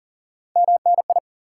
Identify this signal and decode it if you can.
Morse code, heard as a keyed tone, about 37 wpm, 700 Hz. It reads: MNI
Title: Top 90 QSO Elements 40wpm